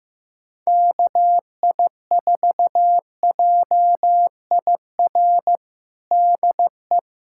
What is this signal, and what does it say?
Morse code, heard as a keyed tone, about 15 wpm, 700 Hz. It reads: KI4JIR DE
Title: USA Callsign DE Callsign - 6 Characters - 15wpm